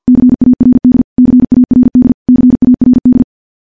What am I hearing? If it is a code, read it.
Morse code, heard as a keyed tone, about 50 wpm, 250 Hz. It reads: CARD CARD CARD
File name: morse / morse-training-50WPM